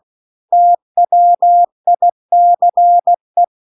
Morse code, heard as a keyed tone, about 16 wpm, 700 Hz. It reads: TWICE
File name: CW-The-Big-Twoheaded-River-2-16WPM